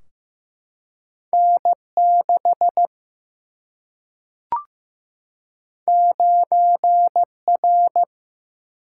Morse code, heard as a keyed tone, about 15 wpm, 700 Hz. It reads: N 6 9 R
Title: Single-Letter Single-Number 15wpm